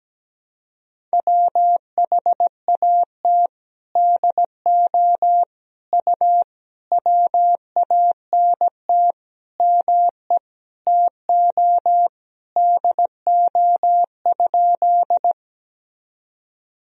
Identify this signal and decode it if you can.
Morse code, heard as a keyed tone, about 17 wpm, 700 Hz. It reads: WHAT DO U WANT ME TO DO?
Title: Sentences from Top 200 Words and Abbreviations 17wpm